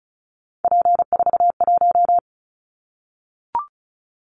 Morse code, heard as a keyed tone, about 34 wpm, 700 Hz. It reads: P41
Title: Callsign Prefixes for Top 100 DXCC - 35wpm